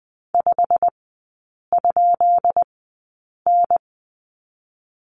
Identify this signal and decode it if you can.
Morse code, heard as a keyed tone, about 20 wpm, spaced as for 8 wpm, 700 Hz. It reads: 5?N
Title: 205 Lesson 33e - Letter j - Three Characters 20wpm